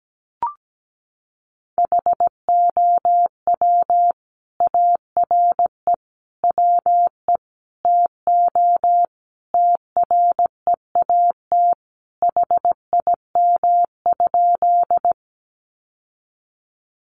Morse code, heard as a keyed tone, about 17 wpm, 700 Hz. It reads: HOW ARE WE TO TREAT HIM?
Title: Sentences from Top 1300 Words 17wpm